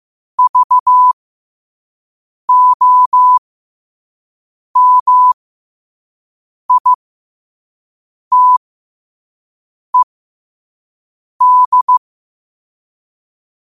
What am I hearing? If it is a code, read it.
Morse code, heard as a keyed tone, about 15 wpm, spaced as for 5 wpm, 1000 Hz. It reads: VOMITED
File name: qotdfiles / 5wpm